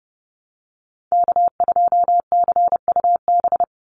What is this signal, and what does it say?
Morse code, heard as a keyed tone, about 30 wpm, 700 Hz. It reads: K2CUB
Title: Call Signs - 30wpm